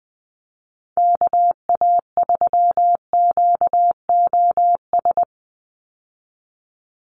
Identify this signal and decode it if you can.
Morse code, heard as a keyed tone, about 20 wpm, 700 Hz. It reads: KA3QOS
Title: USA Call Signs 20wpm